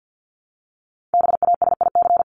Morse code, heard as a keyed tone, about 50 wpm, 700 Hz. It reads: BASIC
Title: Top 501-750 Words 50wpm